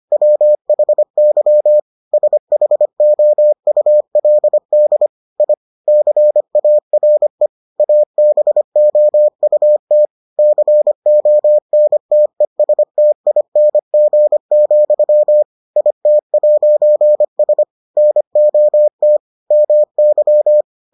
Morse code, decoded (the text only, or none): WHY SHOULD I CARE ABOUT CONTESTING, IT'S NOT MY